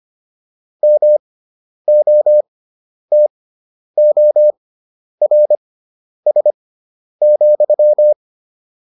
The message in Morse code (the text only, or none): MOTORS,